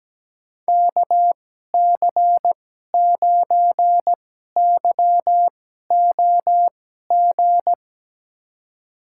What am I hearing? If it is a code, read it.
Morse code, heard as a keyed tone, about 17 wpm, spaced as for 12 wpm, 700 Hz. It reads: KC9YOG
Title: USA Call Signs - 2x Character Spacing - 17wpm